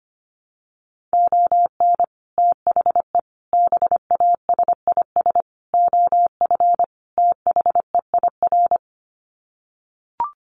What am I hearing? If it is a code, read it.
Morse code, heard as a keyed tone, about 25 wpm, 700 Hz. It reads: ON THE BASIS OF THEIR
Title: Sets of 5 Words Encore 25wpm